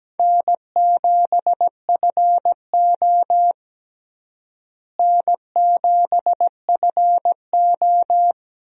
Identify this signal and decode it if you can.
Morse code, heard as a keyed tone, about 17 wpm, 700 Hz. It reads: N7FO N7FO
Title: USA Call Signs - 4 Characters - Repeated 2x - 17wpm